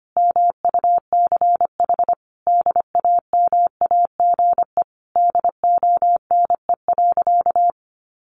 Morse code, heard as a keyed tone, about 25 wpm, 700 Hz. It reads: MUCH DAMAGE DONE.
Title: Sentences from Top 1100 Words 25wpm